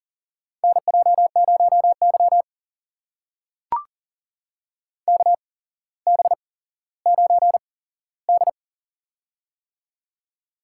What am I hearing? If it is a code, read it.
Morse code, heard as a keyed tone, about 40 wpm, spaced as for 17 wpm, 700 Hz. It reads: NJ0Y KB9D
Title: USA Call Signs - 4 Characters - 8x Character Spacing - 40wpm